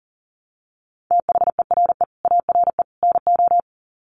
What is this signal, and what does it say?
Morse code, heard as a keyed tone, about 40 wpm, 700 Hz. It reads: THERE ARE NO
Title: Sets of 3 Words 40wpm